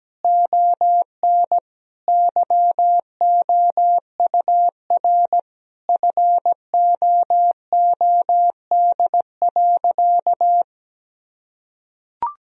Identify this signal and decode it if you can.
Morse code, heard as a keyed tone, about 17 wpm, 700 Hz. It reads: ON YOUR FOOD.
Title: Sentences from Top 1000 Words - Review Part 2 - 17wpm